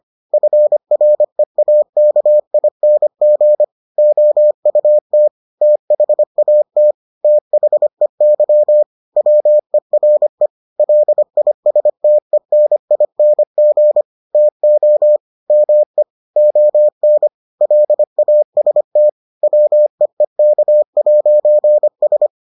Morse code, heard as a keyed tone, about 25 wpm, 600 Hz. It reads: FREAKING OUT THAT THEY WERE LISTENING TO ME ON LAST WEEK'S